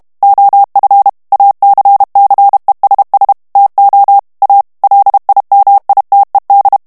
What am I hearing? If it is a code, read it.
Morse code, heard as a keyed tone, about 32 wpm, 800 Hz. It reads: OF ACCESS TO A LIMITED
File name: morse mp3 / ANGLAIS texte clair 3 vitesse 30 mots minute